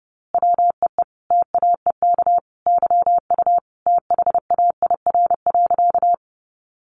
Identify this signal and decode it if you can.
Morse code, heard as a keyed tone, about 30 wpm, 700 Hz. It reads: WEE TAEK YU THAIR.
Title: A Sound Of Thunder 30wpm